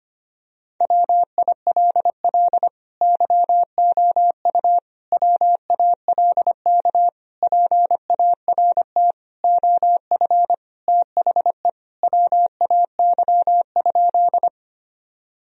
Morse code, heard as a keyed tone, about 25 wpm, 700 Hz. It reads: WILL YOU WALK PART OF THE WAY?
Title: Sentences from Top 1000 Words - Review Part 1 - 25wpm